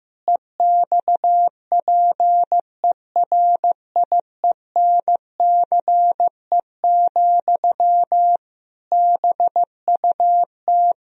Morse code, heard as a keyed tone, about 15 wpm, 700 Hz. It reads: EXPERIENCE, BUT
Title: Sentences from Top 1000 Words - Review Part 4 - 15wpm